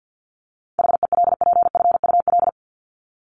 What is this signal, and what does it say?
Morse code, heard as a keyed tone, about 50 wpm, 700 Hz. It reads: HELPFUL
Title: Top 751-1000 Words 50wpm